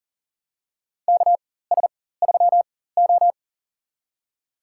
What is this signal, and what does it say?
Morse code, heard as a keyed tone, about 40 wpm, spaced as for 19 wpm, 700 Hz. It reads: KS3O
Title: USA Call Signs - 4 Characters - 4x Character Spacing - 40wpm